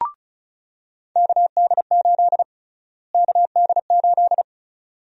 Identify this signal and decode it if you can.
Morse code, heard as a keyed tone, about 36 wpm, 700 Hz. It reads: KD8 KD8
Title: USA Call Sign Prefixes - Repeated 2x - 35wpm